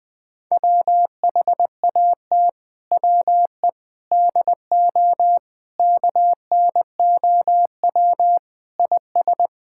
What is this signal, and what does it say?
Morse code, heard as a keyed tone, about 20 wpm, 700 Hz. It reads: WHAT WE DO KNOW IS
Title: Sets of 6 Words - 20wpm